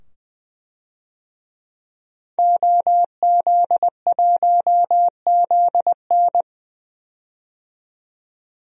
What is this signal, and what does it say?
Morse code, heard as a keyed tone, about 21 wpm, 700 Hz. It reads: OZ1ZN
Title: DXCC-100 Call Signs - 5 Characters - 20wpm